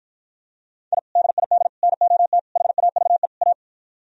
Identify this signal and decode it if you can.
Morse code, heard as a keyed tone, about 52 wpm, 700 Hz. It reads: I DID NOT HAVE A